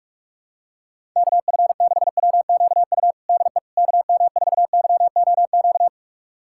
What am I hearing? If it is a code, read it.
Morse code, heard as a keyed tone, about 46 wpm, 700 Hz. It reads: KF6WQU DE KM4YOQ
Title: USA Callsign DE Callsign - 6 Characters - 45wpm